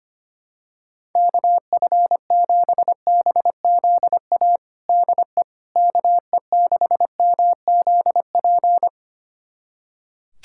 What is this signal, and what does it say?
Morse code, heard as a keyed tone, about 25 wpm, 700 Hz. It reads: KF7BZA DE KE6MZP
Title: USA Callsign DE Callsign - 6 Characters - 25wpm